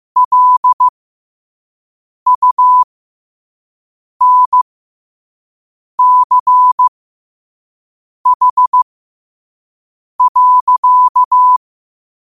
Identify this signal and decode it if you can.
Morse code, heard as a keyed tone, about 15 wpm, spaced as for 5 wpm, 1000 Hz. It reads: LUNCH.
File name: qotdfiles / 5wpm